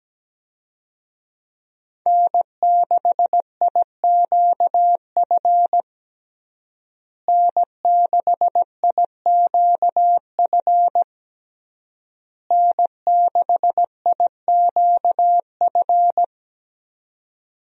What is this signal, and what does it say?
Morse code, heard as a keyed tone, about 17 wpm, 700 Hz. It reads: N6IQF N6IQF N6IQF
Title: USA Call Signs - 5 Characters - Repeated 3x - 17wpm